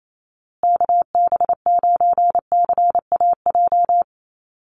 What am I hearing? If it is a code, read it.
Morse code, heard as a keyed tone, about 28 wpm, 700 Hz. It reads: KB9CAJ
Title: USA Call Signs Repeated 2x 28wpm